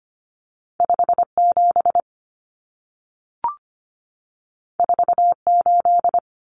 Morse code, heard as a keyed tone, about 25 wpm, 700 Hz. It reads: 57 48
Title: Common Cut Numbers - 2-Digit 25wpm